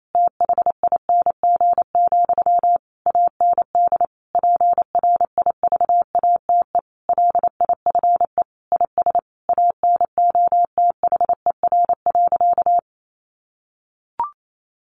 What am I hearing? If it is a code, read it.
Morse code, heard as a keyed tone, about 28 wpm, 700 Hz. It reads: THING, AND PRIVATE LIFE IS ANOTHER.
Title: Sentences from Top 1000 Words 28wpm